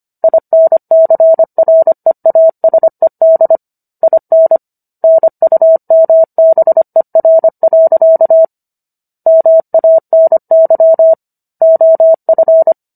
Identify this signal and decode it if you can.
Morse code, heard as a keyed tone, about 25 wpm, 650 Hz. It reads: INCREASED IN NUMBER. MANY OF